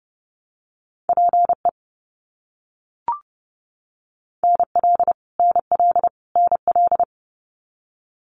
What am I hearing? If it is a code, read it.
Morse code, heard as a keyed tone, about 30 wpm, 700 Hz. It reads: PE NL NL NL
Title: Canadian Provinces and Territories - Repeated 3x - 30wpm